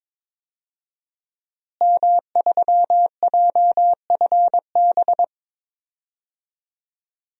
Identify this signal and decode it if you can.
Morse code, heard as a keyed tone, about 22 wpm, 700 Hz. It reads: M3JFB
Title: DXCC-100 Call Signs - 5 Characters - Repeated 2x - 22wpm